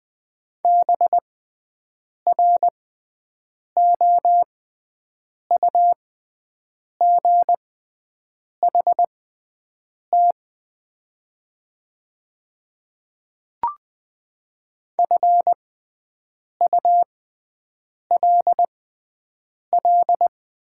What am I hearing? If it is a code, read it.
Morse code, heard as a keyed tone, about 20 wpm, spaced as for 7 wpm, 700 Hz. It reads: BROUGHT FULL
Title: 6x Character Spacing - Top 500 Words - 20wpm